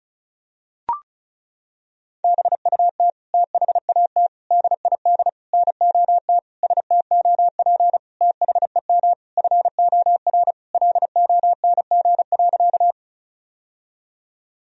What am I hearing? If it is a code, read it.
Morse code, heard as a keyed tone, about 34 wpm, 700 Hz. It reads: BUT THAT DID NOT STOP THEM FOR LONG.
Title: Sentences from Top 1000 Words - Review Part 1 - 35wpm